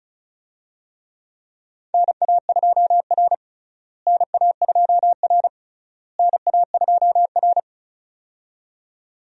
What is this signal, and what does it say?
Morse code, heard as a keyed tone, about 36 wpm, 700 Hz. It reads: NA2R NA2R NA2R
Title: USA Call Signs - 4 Characters - Repeated 3x - 35wpm